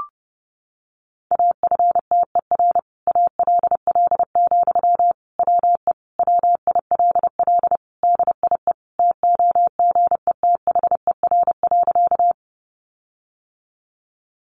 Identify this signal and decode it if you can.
Morse code, heard as a keyed tone, about 30 wpm, 700 Hz. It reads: AFTER ALL, WE WILL DIE TOGETHER.
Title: Sentences from Top 1300 Words 30wpm